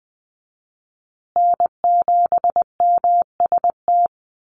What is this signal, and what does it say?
Morse code, heard as a keyed tone, about 20 wpm, 700 Hz. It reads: N7MST
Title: USA Call Signs Repeated 2x 20wpm